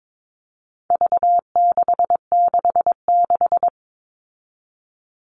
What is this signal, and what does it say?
Morse code, heard as a keyed tone, about 22 wpm, 700 Hz. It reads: V666
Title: All Cut Numbers - 4-Digit 22wpm